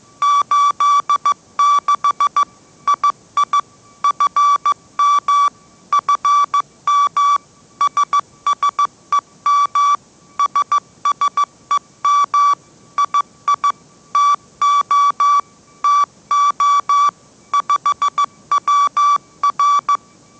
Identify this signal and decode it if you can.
Morse code, heard as a keyed tone, about 15 wpm, 1150 Hz. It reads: 86 II FM FM SSEM SSEM II TO TO 5WR
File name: mrx / mrx065 16wpm